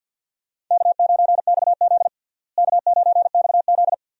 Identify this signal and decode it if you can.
Morse code, heard as a keyed tone, about 46 wpm, 700 Hz. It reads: K9XZ K9XZ
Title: USA Call Signs - 4 Characters - Repeated 3x - 50wpm